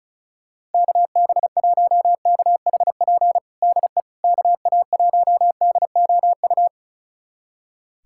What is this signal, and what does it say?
Morse code, heard as a keyed tone, about 36 wpm, 700 Hz. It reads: KB1KHP DE KA1DOU
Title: USA Callsign DE Callsign 35wpm